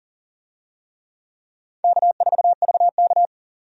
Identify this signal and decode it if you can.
Morse code, heard as a keyed tone, about 40 wpm, 700 Hz. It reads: K4VK
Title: USA Call Signs - 4 Characters - 40wpm